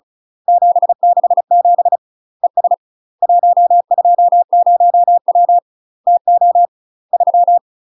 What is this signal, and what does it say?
Morse code, heard as a keyed tone, about 35 wpm, 700 Hz. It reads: 767 ES 120W TO 3
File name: UK MASTER QSOs TRAINING / 35WPM / UK MASTER QSOs 0016